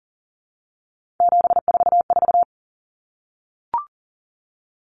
Morse code, heard as a keyed tone, about 40 wpm, 700 Hz. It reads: 744
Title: Common Cut Numbers - 3-Digit 40wpm